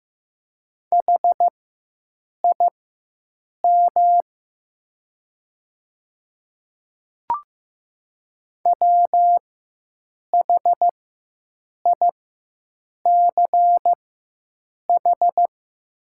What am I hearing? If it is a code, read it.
Morse code, heard as a keyed tone, about 15 wpm, spaced as for 7 wpm, 700 Hz. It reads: HIM WHICH
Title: 4x Character Spacing - Top 100 Words - 15wpm